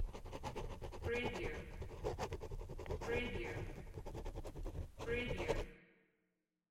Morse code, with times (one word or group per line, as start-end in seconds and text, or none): none